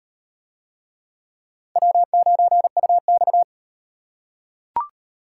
Morse code, heard as a keyed tone, about 38 wpm, 700 Hz. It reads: W9UX